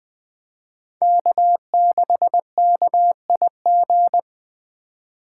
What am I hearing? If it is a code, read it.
Morse code, heard as a keyed tone, about 20 wpm, 700 Hz. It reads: K6KIG
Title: Speed Racing - USA 1x3 Call Signs 17wpm